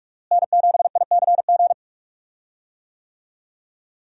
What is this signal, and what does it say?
Morse code, heard as a keyed tone, about 46 wpm, 700 Hz. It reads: N7ICG
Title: USA Call Signs - 5 Characters - Repeated 2x - 45wpm